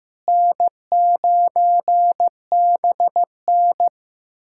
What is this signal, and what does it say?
Morse code, heard as a keyed tone, about 15 wpm, 700 Hz. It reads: N9BN
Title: USA Callsign DE Callsign - 4 Characters - 15wpm